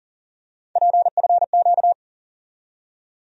Morse code, heard as a keyed tone, about 40 wpm, 700 Hz. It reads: PFQ
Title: USA Call Sign Postfixes - 3 Letters 40wpm